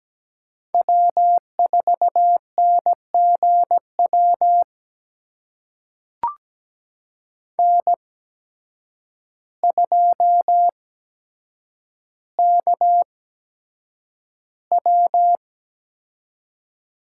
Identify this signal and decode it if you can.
Morse code, heard as a keyed tone, about 17 wpm, spaced as for 8 wpm, 700 Hz. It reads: W4NGW N2KW
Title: USA Call Signs - 5 Characters - 8x Character Spacing - 17wpm